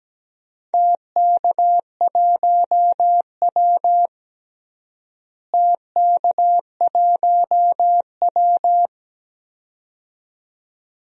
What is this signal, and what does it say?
Morse code, heard as a keyed tone, about 17 wpm, 700 Hz. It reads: TK1W TK1W
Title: DXCC-100 Call Signs - 4 Characters - Repeated 2x - 17wpm